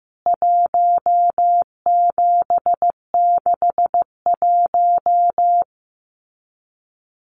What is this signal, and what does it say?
Morse code, heard as a keyed tone, about 15 wpm, 700 Hz. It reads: 1761
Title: CWT Exchange 15wpm